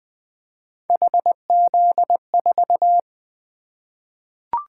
Morse code, heard as a keyed tone, about 20 wpm, 700 Hz. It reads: HZ4
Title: Three Letter-Number Combinations - 20wpm